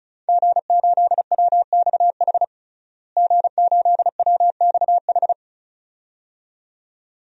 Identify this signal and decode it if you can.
Morse code, heard as a keyed tone, about 36 wpm, 700 Hz. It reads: G8WXH G8WXH
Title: DXCC-100 Call Signs - 5 Characters - Repeated 2x - 35wpm